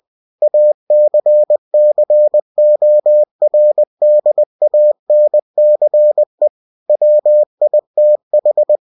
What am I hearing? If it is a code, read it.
Morse code, heard as a keyed tone, about 20 wpm, 600 Hz. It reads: ACCORDANCE WITH
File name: cw-books / winnie-the-pooh / winnie-20wpm-page-0117